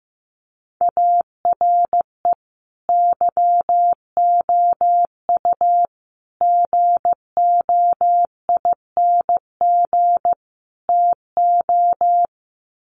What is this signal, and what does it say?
Morse code, heard as a keyed tone, about 15 wpm, 700 Hz. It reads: ARE YOU GOING TO
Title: Sets of 4 Words 15wpm